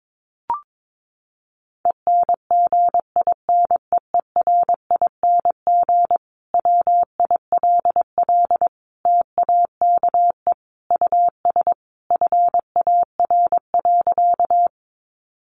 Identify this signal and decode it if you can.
Morse code, heard as a keyed tone, about 22 wpm, 700 Hz. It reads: ENGINEERING WILL TAKE US FAR.
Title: Sentences from Top 1500 Words 22wpm